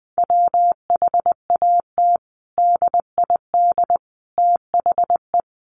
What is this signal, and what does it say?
Morse code, heard as a keyed tone, about 20 wpm, 700 Hz. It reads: WHAT DID THE
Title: Sentences from Top 500 Words 20wpm